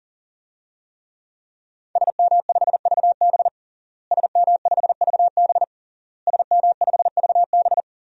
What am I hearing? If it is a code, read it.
Morse code, heard as a keyed tone, about 40 wpm, 700 Hz. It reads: SM5VB SM5VB SM5VB
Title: DXCC-100 Call Signs - 5 Characters - Repeated 3x - 40wpm